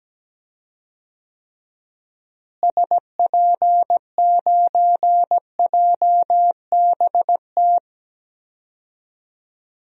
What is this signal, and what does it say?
Morse code, heard as a keyed tone, about 17 wpm, 700 Hz. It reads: SP9JBT
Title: DXCC-100 Call Signs - 6 Characters - Repeated 3x - 17wpm